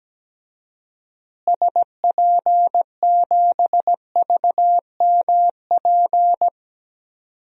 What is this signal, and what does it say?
Morse code, heard as a keyed tone, about 17 wpm, 700 Hz. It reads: SP7VMP
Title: DXCC-100 Call Signs - 6 Characters - Repeated 3x - 17wpm